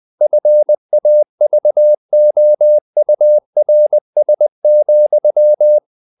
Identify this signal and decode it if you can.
Morse code, heard as a keyed tone, about 20 wpm, 600 Hz. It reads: FAVOURS,
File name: the old man and the sea / the old man and the sea part 1 20wpm